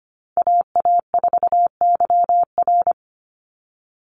Morse code, heard as a keyed tone, about 25 wpm, 700 Hz. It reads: AA4YR
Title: USA Call Signs 25wpm